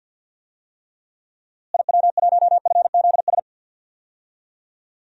Morse code, heard as a keyed tone, about 48 wpm, 700 Hz. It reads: IW1FZS
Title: DXCC-100 Call Signs - 6 Characters - Repeated 2x - 50wpm